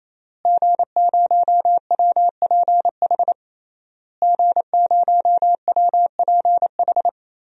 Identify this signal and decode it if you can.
Morse code, heard as a keyed tone, about 28 wpm, 700 Hz. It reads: G0WPH G0WPH
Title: DXCC-100 Call Signs - 5 Characters - Repeated 3x - 28wpm